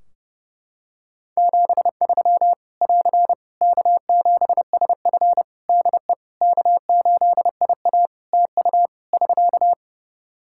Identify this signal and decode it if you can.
Morse code, heard as a keyed tone, about 30 wpm, 700 Hz. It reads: 73 <AR> K7SF DE K8IA TU <SK>
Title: Practice QSO - Exchange 4 - Ending - 30wpm